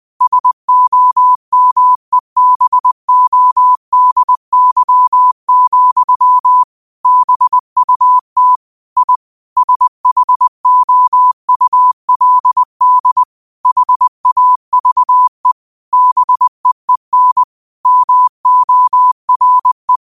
Morse code, heard as a keyed tone, about 20 wpm, 1000 Hz. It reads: SOMEBODY, BUT I SHOULD HAVE BEEN MORE